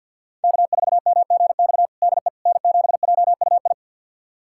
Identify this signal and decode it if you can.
Morse code, heard as a keyed tone, about 48 wpm, 700 Hz. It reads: K4MGX DE N7JUI
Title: USA Callsign DE Callsign - 5 Characters - 50wpm